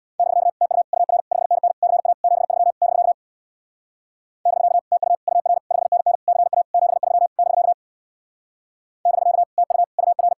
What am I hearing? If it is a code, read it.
Morse code, heard as a keyed tone, about 64 wpm, 700 Hz. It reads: <BK> TU UR 5NN CA 73 <BK> <BK> TU UR 5NN CA 73 <BK> <BK> TU UR